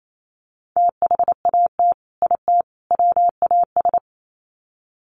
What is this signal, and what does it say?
Morse code, heard as a keyed tone, about 28 wpm, 700 Hz. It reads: THAT IT WAS
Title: Sets of 3 Words 28wpm